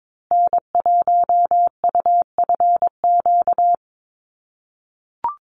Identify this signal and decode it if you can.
Morse code, heard as a keyed tone, about 22 wpm, 700 Hz. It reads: N1UFQ
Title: USA Call Signs Repeated 3x 22wpm